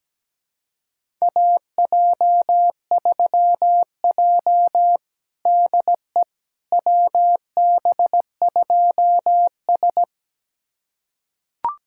Morse code, heard as a keyed tone, about 17 wpm, 700 Hz. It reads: AJ3J DE WB2S
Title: USA Callsign DE Callsign - 4 Characters - 17wpm